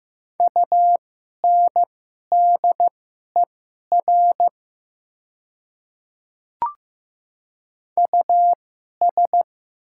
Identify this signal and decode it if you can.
Morse code, heard as a keyed tone, about 15 wpm, spaced as for 11 wpm, 700 Hz. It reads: UNDER US
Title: 2x Character Spacing - Top 200 Words - 15wpm